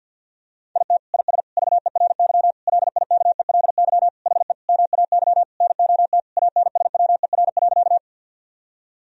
Morse code, heard as a keyed tone, about 51 wpm, 700 Hz. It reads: IT IS VERY LIKELY HE MAY NOT ANSWER.
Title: Sentences from Top 1000 Words - Review Part 4 - 50wpm